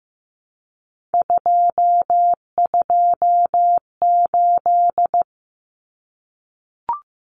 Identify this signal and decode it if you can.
Morse code, heard as a keyed tone, about 15 wpm, 700 Hz. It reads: 228
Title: Common Cut Numbers - 3-Digit 15wpm